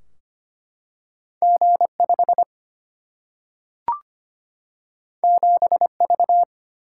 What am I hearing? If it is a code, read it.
Morse code, heard as a keyed tone, about 25 wpm, 700 Hz. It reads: G5 7V
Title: Single-Letter Single-Number 25wpm